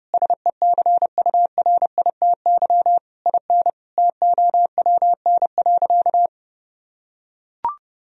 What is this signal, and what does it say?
Morse code, heard as a keyed tone, about 29 wpm, 700 Hz. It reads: SECURITY IN TOWN.
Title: Sentences from Top 1000 Words - Review Part 4 - 30wpm